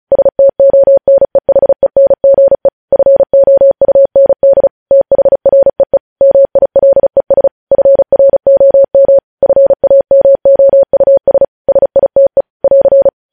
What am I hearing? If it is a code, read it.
Morse code, heard as a keyed tone, about 36 wpm, 550 Hz. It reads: STONEHENGE FOUND THREE MILES FROM FAMOUS SITE <AR>
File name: podcast / News headlines-1x-35x35